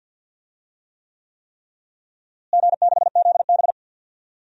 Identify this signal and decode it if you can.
Morse code, heard as a keyed tone, about 49 wpm, 700 Hz. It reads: G6ZB
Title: DXCC-100 Call Signs - 4 Characters - Repeated 3x - 50wpm